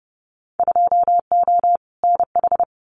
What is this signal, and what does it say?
Morse code, heard as a keyed tone, about 30 wpm, 700 Hz. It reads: 2O NH
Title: Winter Field Day Exchanges 30wpm